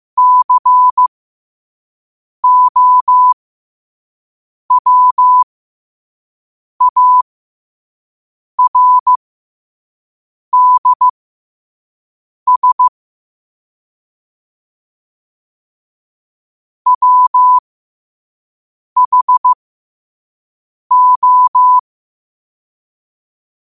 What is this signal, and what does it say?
Morse code, heard as a keyed tone, about 15 wpm, spaced as for 5 wpm, 1000 Hz. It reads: COWARDS WHO
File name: qotdfiles / 5wpm